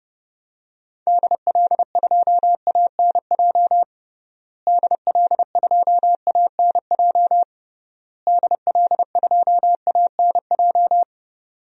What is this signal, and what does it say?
Morse code, heard as a keyed tone, about 31 wpm, 700 Hz. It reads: DL2ANJ DL2ANJ DL2ANJ
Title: DXCC-100 Call Signs - 6 Characters - Repeated 3x - 30wpm